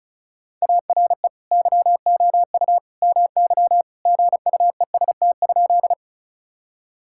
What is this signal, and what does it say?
Morse code, heard as a keyed tone, about 36 wpm, 700 Hz. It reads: ARE YOU MY GUEST?
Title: Sentences from Top 1600 Words 35wpm